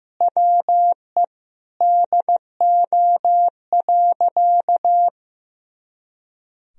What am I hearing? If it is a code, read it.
Morse code, heard as a keyed tone, about 16 wpm, 700 Hz. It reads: WE DO.
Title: Sentences from Top 100 Words and Abbreviations 15wpm